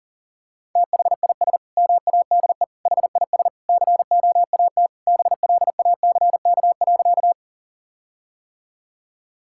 Morse code, heard as a keyed tone, about 37 wpm, 700 Hz. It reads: THIS MADE HIS COAT BLACK.
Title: Sentences from Top 1600 Words 40wpm